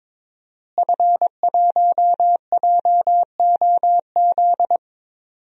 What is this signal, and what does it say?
Morse code, heard as a keyed tone, about 22 wpm, 700 Hz. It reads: F1JOZ
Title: DXCC-100 Call Signs - 5 Characters - Repeated 3x - 22wpm